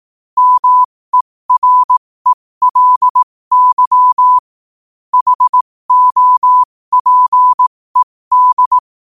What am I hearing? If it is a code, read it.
Morse code, heard as a keyed tone, about 18 wpm, spaced as for 15 wpm, 1000 Hz. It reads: MERELY HOPED
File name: qotdfiles / 15wpm